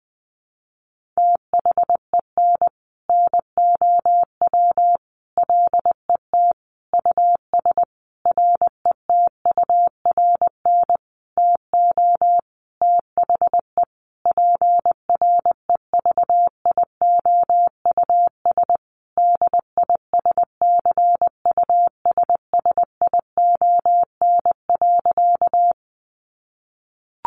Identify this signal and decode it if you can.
Morse code, heard as a keyed tone, about 20 wpm, 700 Hz. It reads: THEN NOW LET US RETURN TO THE PREVIOUS DISCUSSION.
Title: Sentences from Top 1500 Words 20wpm